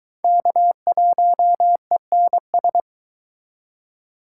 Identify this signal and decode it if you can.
Morse code, heard as a keyed tone, about 23 wpm, 700 Hz. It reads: K1ENS